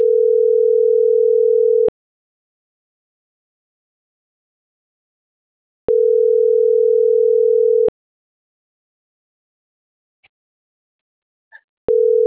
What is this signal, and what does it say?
Ringback tone (North America), 440 + 480 Hz.
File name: CENTRAL SAGOT / out isai.hernandez - .90219